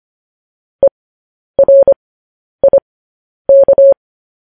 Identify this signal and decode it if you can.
Morse code, heard as a keyed tone, about 25 wpm, spaced as for 10 wpm, 550 Hz. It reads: ERIK
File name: podcast / News headlines-1x-25x10